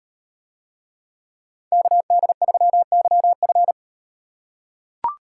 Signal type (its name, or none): Morse code